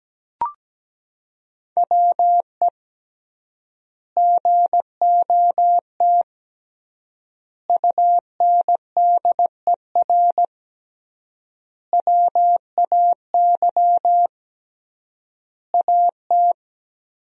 Morse code, heard as a keyed tone, about 17 wpm, 700 Hz. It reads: WE GOT UNDER WAY AT
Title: 3x Word-Spacing - Sentences from Top 300 Words - 17wpm